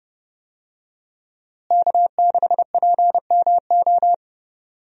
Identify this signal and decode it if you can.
Morse code, heard as a keyed tone, about 30 wpm, 700 Hz. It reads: K6PMO
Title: USA Call Signs - 5 Characters - Repeated 2x - 30wpm